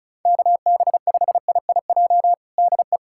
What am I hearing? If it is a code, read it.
Morse code, heard as a keyed tone, about 36 wpm, 700 Hz. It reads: KB5IIJ DE
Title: USA Callsign DE Callsign - 6 Characters - 35wpm